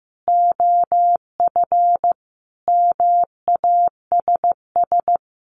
Morse code, heard as a keyed tone, about 15 wpm, 700 Hz. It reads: OF MASS
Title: Sets of 5 Words Encore 15wpm